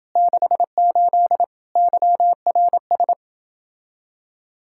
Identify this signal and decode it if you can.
Morse code, heard as a keyed tone, about 30 wpm, 700 Hz. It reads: 68 YRS